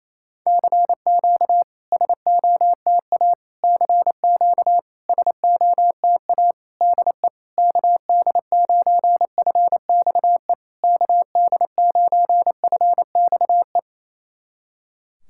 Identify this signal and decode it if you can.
Morse code, heard as a keyed tone, about 28 wpm, 690 Hz. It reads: CQ SOTA CQ SOTA DE KD9FXE KD9FXE
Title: SOTA QSO Practice 28wpm